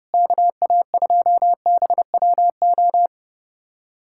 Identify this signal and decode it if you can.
Morse code, heard as a keyed tone, about 30 wpm, 700 Hz. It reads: KA2BWO
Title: USA Callsign DE Callsign - 6 Characters - 30wpm